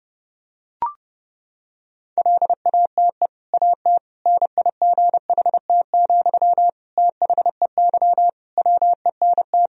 Morse code, heard as a keyed tone, about 30 wpm, 700 Hz. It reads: LATE AT NIGHT, THEY WENT
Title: Sentences from Top 1000 Words - Review Part 3 - 30wpm